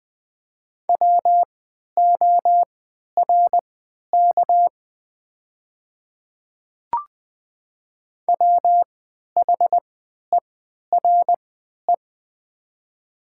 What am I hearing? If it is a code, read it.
Morse code, heard as a keyed tone, about 20 wpm, spaced as for 11 wpm, 690 Hz. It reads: WORK WHERE
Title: 3x Character Spacing - Top 300 Words - 20wpm